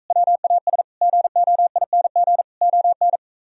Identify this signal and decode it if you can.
Morse code, heard as a keyed tone, about 45 wpm, 700 Hz. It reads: WAS GOING ON